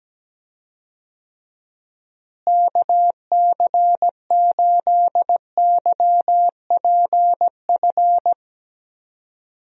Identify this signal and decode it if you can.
Morse code, heard as a keyed tone, about 17 wpm, 700 Hz. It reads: KC8YPF
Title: USA Call Signs - 6 Characters - Repeated 3x - 17wpm